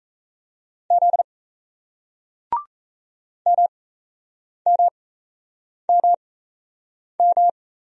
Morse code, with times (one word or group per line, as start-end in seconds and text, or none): 0.9-1.2 Z
3.5-7.5 MMMM